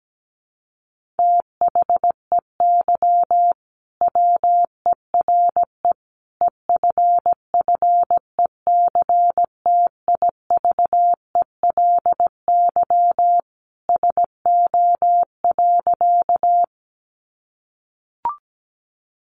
Morse code, heard as a keyed tone, about 17 wpm, 700 Hz. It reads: THEY WERE EFFECTIVELY SO.
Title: Sentences from Top 1400 Words 17wpm